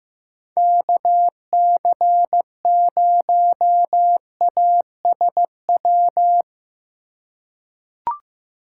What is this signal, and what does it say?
Morse code, heard as a keyed tone, about 15 wpm, 700 Hz. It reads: KC0ASW
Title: USA Call Signs - 6 Characters - 4x Character Spacing - 15wpm